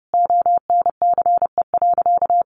Morse code, heard as a keyed tone, about 30 wpm, 700 Hz. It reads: ONCE.
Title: Sentences from Top 1300 Words 30wpm